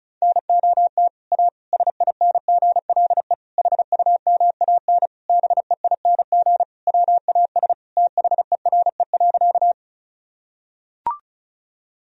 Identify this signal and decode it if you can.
Morse code, heard as a keyed tone, about 34 wpm, 700 Hz. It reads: NOT A SINGLE HUMAN BEING WAS THERE.
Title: Sentences from Top 1000 Words - Review Part 4 - 35wpm